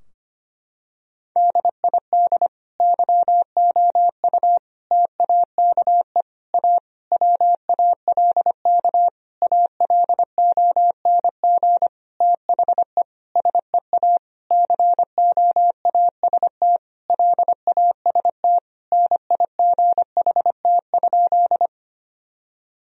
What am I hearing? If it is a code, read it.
Morse code, heard as a keyed tone, about 25 wpm, 700 Hz. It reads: DID YOU TAKE A WALK ALONG THE SEA COAST LAST NIGHT?
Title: Sentences from Top 1000 Words - Review Part 4 - 25wpm